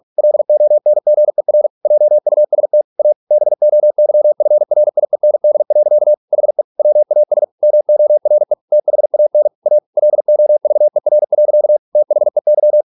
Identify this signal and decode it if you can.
Morse code, heard as a keyed tone, about 46 wpm, 600 Hz. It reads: LONGER JUST A BOYFRIEND. HE WAS MORE THAN A LOVER. THEY